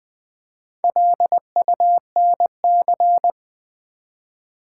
Morse code, heard as a keyed tone, about 20 wpm, 700 Hz. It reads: LUNC
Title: Build A Word - 1 Letter At A Time - 3 to 5-Letter Words - 20wpm